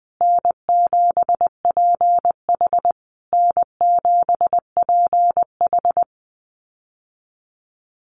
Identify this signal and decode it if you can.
Morse code, heard as a keyed tone, about 20 wpm, 700 Hz. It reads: N7PH N7PH
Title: USA Call Signs Repeated 3x 20wpm